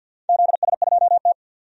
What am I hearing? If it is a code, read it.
Morse code, heard as a keyed tone, about 52 wpm, 700 Hz. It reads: ZS2T
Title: DXCC-100 Call Signs - 4 Characters - 50wpm